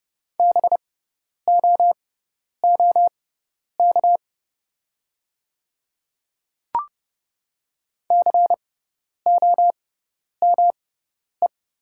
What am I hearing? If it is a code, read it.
Morse code, heard as a keyed tone, about 30 wpm, spaced as for 10 wpm, 690 Hz. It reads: BOOK COME
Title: 6x Character Spacing - Top 500 Words - 30wpm